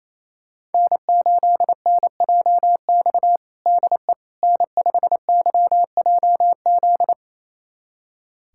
Morse code, heard as a keyed tone, about 28 wpm, 700 Hz. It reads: N8NJX DE N5YJZ
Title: USA Callsign DE Callsign - 5 Characters - 28wpm